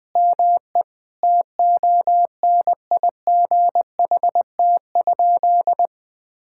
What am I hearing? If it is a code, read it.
Morse code, heard as a keyed tone, about 21 wpm, 700 Hz. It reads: ME TONIGHT?
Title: Sentences from Top 1600 Words 20wpm